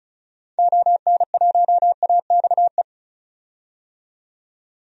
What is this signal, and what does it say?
Morse code, heard as a keyed tone, about 36 wpm, 700 Hz. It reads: ON1AXE
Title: DXCC-100 Call Signs - 6 Characters - 35wpm